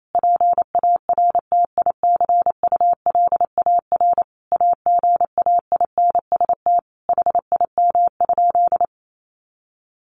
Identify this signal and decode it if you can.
Morse code, heard as a keyed tone, about 28 wpm, 700 Hz. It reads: PARTICULAR AGAINST HIM?
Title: Sentences from Top 1000 Words 28wpm